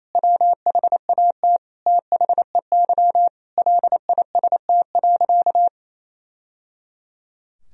Morse code, heard as a keyed tone, about 28 wpm, 700 Hz. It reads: WHAT THEY LIST.
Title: Sentences from Top 1000 Words - Review Part 2 - 28wpm